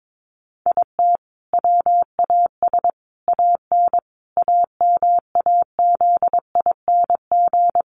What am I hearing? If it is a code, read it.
Morse code, heard as a keyed tone, about 22 wpm, 700 Hz. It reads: IT WAS AN AMAZING
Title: Sentences from Top 1500 Words 22wpm